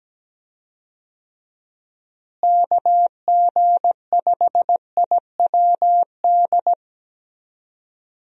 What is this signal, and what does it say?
Morse code, heard as a keyed tone, about 17 wpm, 700 Hz. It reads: KG5IWD
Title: USA Call Signs - 6 Characters - Repeated 2x - 17wpm